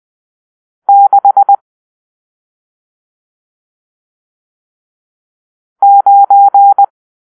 Morse code, heard as a keyed tone, about 20 wpm, 800 Hz. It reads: 6 9
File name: arrl / Grp-7 Drill-1